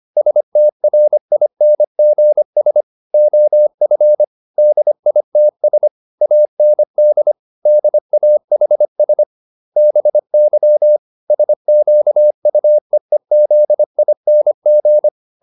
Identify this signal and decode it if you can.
Morse code, heard as a keyed tone, about 25 wpm, 600 Hz. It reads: STRINGS OF DITS AND DAHS BY SQUEEZING
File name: SS315 25